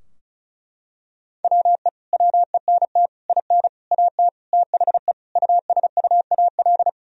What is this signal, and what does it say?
Morse code, heard as a keyed tone, about 34 wpm, 700 Hz. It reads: WE WENT IN AT THE USUAL
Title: Sentences from Top 1000 Words - Review Part 2 - 35wpm